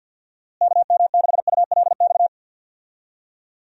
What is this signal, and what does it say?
Morse code, heard as a keyed tone, about 50 wpm, 700 Hz. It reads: KM6ULX
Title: USA Call Signs - 6 Characters - Repeated 2x - 50wpm